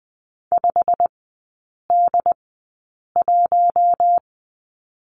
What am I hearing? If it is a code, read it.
Morse code, heard as a keyed tone, about 20 wpm, spaced as for 8 wpm, 700 Hz. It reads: 5D1
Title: 152 Lesson 25e - Letter p - Three Characters 20wpm